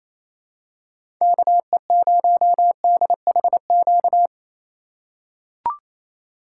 Morse code, heard as a keyed tone, about 28 wpm, 700 Hz. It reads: KE0DHQ
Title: USA Call Signs - 6 Characters - 2x Character Spacing - 28wpm